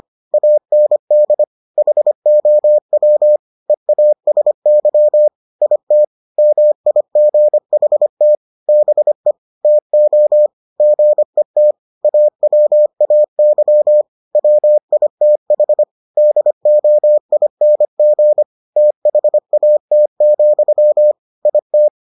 Morse code, heard as a keyed tone, about 25 wpm, 600 Hz. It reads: AND HOW EASY IT MIGHT BE TO GET AWAY WITH DOING THAT, IT